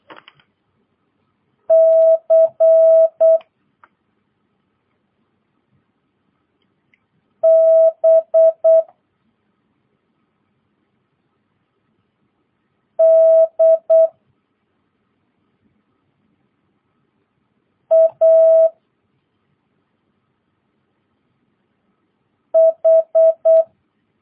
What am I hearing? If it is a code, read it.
Morse code, heard as a keyed tone, about 8 wpm, 650 Hz. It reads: C B D A H
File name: jh8cbh / 2022cw